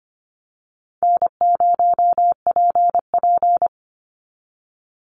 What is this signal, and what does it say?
Morse code, heard as a keyed tone, about 25 wpm, 700 Hz. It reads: N0PP
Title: Easy Call Signs Repeated 2x - 25wpm